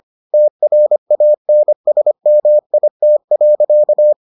Morse code, heard as a keyed tone, about 25 wpm, 600 Hz. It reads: TRANSMIT.